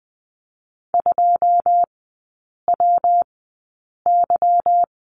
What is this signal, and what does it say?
Morse code, heard as a keyed tone, about 20 wpm, spaced as for 8 wpm, 700 Hz. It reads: 2WY
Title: 166 Lesson 27e - Number 7 - Three Characters 20wpm